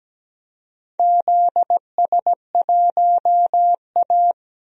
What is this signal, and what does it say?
Morse code, heard as a keyed tone, about 17 wpm, 700 Hz. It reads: ZS1A
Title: DXCC-100 Call Signs - 4 Characters - 17wpm